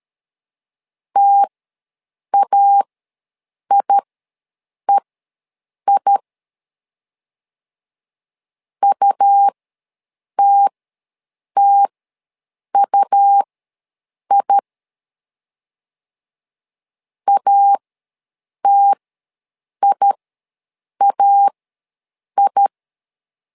Morse code, heard as a keyed tone, about 13 wpm, spaced as for 7 wpm, 800 Hz. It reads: TAIEI UTTUI ATIAI
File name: LEZIONI / 2 UHDN / 2 U ws 12wpm 1655